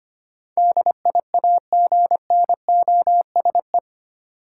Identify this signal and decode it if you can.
Morse code, heard as a keyed tone, about 25 wpm, 700 Hz. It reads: DIAGNOSE
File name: Eight Letter Words - 25wpm